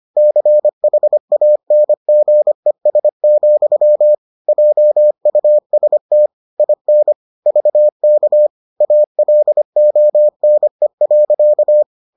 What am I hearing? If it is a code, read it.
Morse code, heard as a keyed tone, about 25 wpm, 600 Hz. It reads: CHANGES, JUST IN VK ALONE.